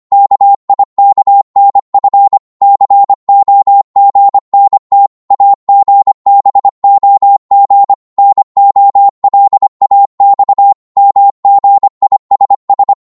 Morse code, heard as a keyed tone, about 25 wpm, 800 Hz. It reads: KIKNF COGNT AGBOG NOLAX MGISS